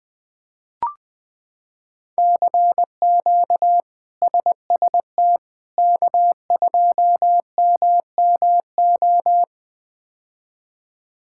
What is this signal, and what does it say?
Morse code, heard as a keyed tone, about 20 wpm, 700 Hz. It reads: CQ SST K2MMO
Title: K1USN SST QSO Practice - 20wpm